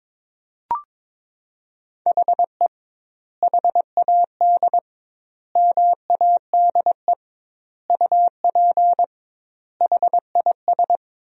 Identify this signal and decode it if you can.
Morse code, heard as a keyed tone, about 22 wpm, 700 Hz. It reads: HE HAD MADE UP HIS
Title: 2x Word-Spacing - Sentences from Top 500 Words - 22wpm